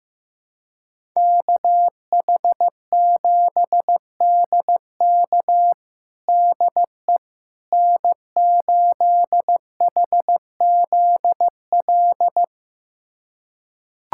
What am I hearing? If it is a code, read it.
Morse code, heard as a keyed tone, about 15 wpm, 700 Hz. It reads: KH7DK DE N8HZL
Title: USA Callsign DE Callsign - 5 Characters - 15wpm